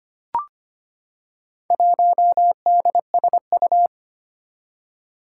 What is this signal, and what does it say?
Morse code, heard as a keyed tone, about 25 wpm, 700 Hz. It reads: 1DSU
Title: Four Challenging Letter-Number Combinations 25wpm